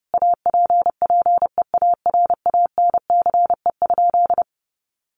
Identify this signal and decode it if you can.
Morse code, heard as a keyed tone, about 30 wpm, 700 Hz. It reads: APPEARANCE?
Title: Sentences from Top 1100 Words 30wpm